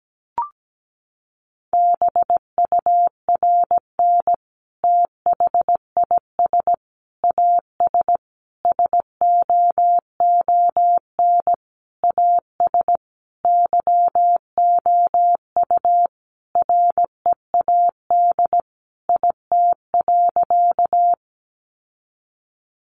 Morse code, heard as a keyed tone, about 17 wpm, 700 Hz. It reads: BURN THIS AS SOON AS YOU READ IT.
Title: Sentences from Top 1300 Words 17wpm